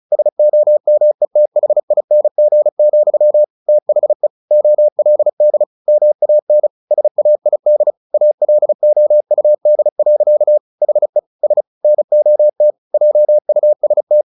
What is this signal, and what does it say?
Morse code, heard as a keyed tone, about 34 wpm, 600 Hz. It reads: SOMETHING, THE OLD MAN SAID ALOUD. HE S NOT JUST
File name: the old man and the sea / the old man and the sea 1 35wpm